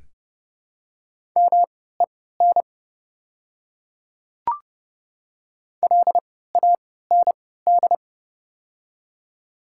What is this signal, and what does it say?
Morse code, heard as a keyed tone, about 30 wpm, spaced as for 17 wpm, 700 Hz. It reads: MEN LAND
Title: 3x Character Spacing - Top 200 Words - 30wpm